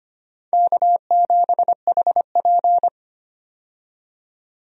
Morse code, heard as a keyed tone, about 25 wpm, 700 Hz. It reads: K7HP
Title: USA Call Signs - 4 Characters - 25wpm